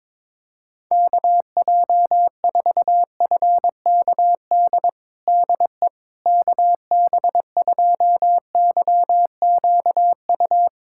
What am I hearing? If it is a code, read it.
Morse code, heard as a keyed tone, about 21 wpm, 700 Hz. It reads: KJ4FKD DE KB2YQU
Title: USA Callsign DE Callsign 22wpm